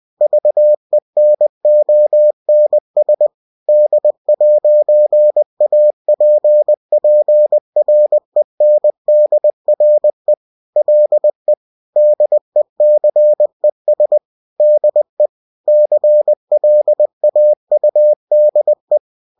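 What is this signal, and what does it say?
Morse code, heard as a keyed tone, about 20 wpm, 600 Hz. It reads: VENONS D'APPRENDRE LE DECES DE CLAUDE
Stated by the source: Bulletin sent in CW.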